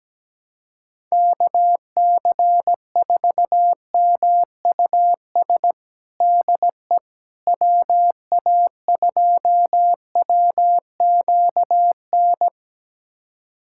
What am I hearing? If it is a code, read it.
Morse code, heard as a keyed tone, about 17 wpm, 700 Hz. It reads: KC4MUS DE WA2WQN
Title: USA Callsign DE Callsign 17wpm